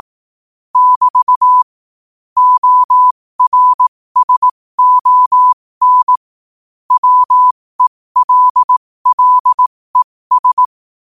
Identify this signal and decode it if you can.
Morse code, heard as a keyed tone, about 18 wpm, spaced as for 15 wpm, 1000 Hz. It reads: <BT> ORSON WELLES
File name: qotdfiles / 15wpm